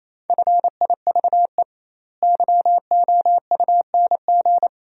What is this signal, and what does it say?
Morse code, heard as a keyed tone, about 28 wpm, 700 Hz. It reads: FIVE YOUNG
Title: 2x Word-Spacing - Sentences from Top 400 Words - 28wpm